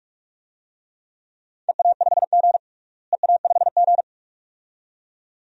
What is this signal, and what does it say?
Morse code, heard as a keyed tone, about 42 wpm, 700 Hz. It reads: EA5G EA5G
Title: DXCC-100 Call Signs - 4 Characters - Repeated 2x - 45wpm